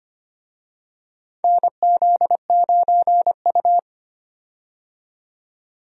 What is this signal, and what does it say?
Morse code, heard as a keyed tone, about 25 wpm, 700 Hz. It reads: NZ9U
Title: USA Call Signs - 4 Characters - 25wpm